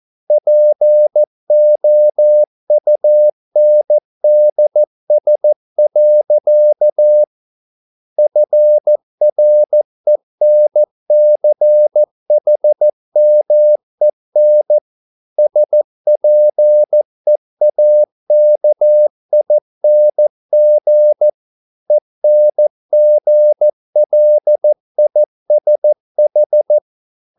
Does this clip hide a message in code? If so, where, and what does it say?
Yes: Morse code from 0.3 to 26.8 s, heard as a keyed tone, about 14 wpm, 600 Hz. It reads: POUNDS. FRENCHMEN SPEAKING ENGLISH